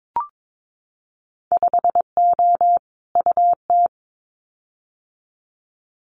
Morse code, heard as a keyed tone, about 22 wpm, 700 Hz. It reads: 5O UT
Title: Winter Field Day Exchanges 22wpm